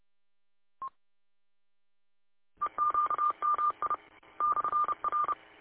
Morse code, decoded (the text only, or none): EQMI /R